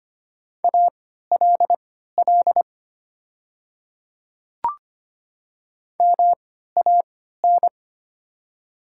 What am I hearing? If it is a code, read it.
Morse code, heard as a keyed tone, about 25 wpm, spaced as for 14 wpm, 690 Hz. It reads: ALL MAN
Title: 3x Character Spacing - Top 500 Words - 25wpm